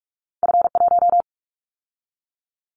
Morse code, heard as a keyed tone, about 46 wpm, 700 Hz. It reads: F1
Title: Callsign Prefixes for Top 50 DXCC - 45wpm